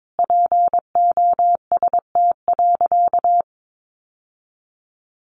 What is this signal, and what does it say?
Morse code, heard as a keyed tone, about 22 wpm, 700 Hz. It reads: POST.
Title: Sentences from Top 1000 Words 22wpm